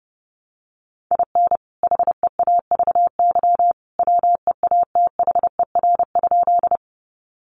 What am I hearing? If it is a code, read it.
Morse code, heard as a keyed tone, about 30 wpm, 700 Hz. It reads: IN HEAVY WEATHER?
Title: Sentences from Top 1100 Words 30wpm